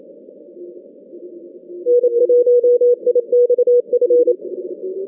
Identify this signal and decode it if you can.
Morse code, heard as a keyed tone, about 28 wpm, 500 Hz. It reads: N1IXF
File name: 2015 cq ww cw / 2015 cqwww k5tr - 0-18